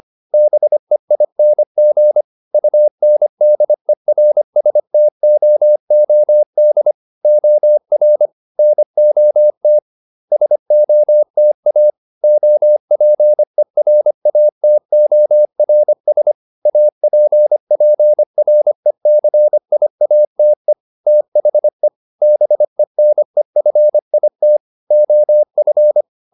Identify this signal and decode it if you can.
Morse code, heard as a keyed tone, about 25 wpm, 600 Hz. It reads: BEING UNDERSTOOD OR NOT SOTA OPERATORS APPRECIATE THE BENEFIT OF